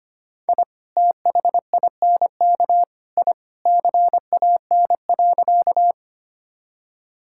Morse code, heard as a keyed tone, about 25 wpm, 700 Hz. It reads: I THINK I CAN.
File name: practice / 25wpm / Sentences with 3-Word Phrases - 25wpm